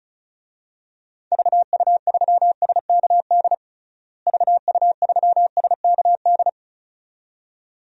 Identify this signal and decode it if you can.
Morse code, heard as a keyed tone, about 34 wpm, 700 Hz. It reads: VU3SKD VU3SKD
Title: DXCC-100 Call Signs - 6 Characters - Repeated 2x - 35wpm